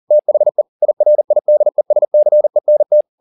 Morse code, heard as a keyed tone, about 40 wpm, 600 Hz. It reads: THE IRIDESCENT